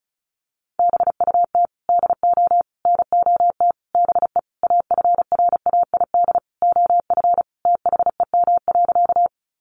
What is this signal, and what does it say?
Morse code, heard as a keyed tone, about 34 wpm, 700 Hz. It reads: BUT DO NOT BE AFRAID OF THEM.
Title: Sentences from Top 1500 Words 35wpm